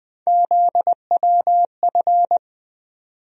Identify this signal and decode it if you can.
Morse code, heard as a keyed tone, about 20 wpm, 700 Hz. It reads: ZWF
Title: USA Call Sign Postfixes - 3 Letters 20wpm